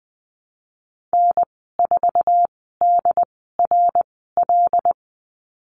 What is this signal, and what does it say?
Morse code, heard as a keyed tone, about 20 wpm, spaced as for 14 wpm, 700 Hz. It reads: N4DRL
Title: 147 Lesson 24g - Letter y - Call Signs 20wpm